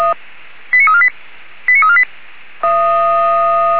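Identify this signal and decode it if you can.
Five-tone selective call (ZVEI), heard as quick runs of five tones, tones of about 70 ms. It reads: 89348 89348
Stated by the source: ZVEI five-tone call 89347 received on SDR.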